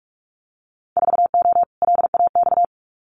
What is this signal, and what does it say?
Morse code, heard as a keyed tone, about 46 wpm, 700 Hz. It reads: 4O LAX
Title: Winter Field Day Exchanges Encore 45wpm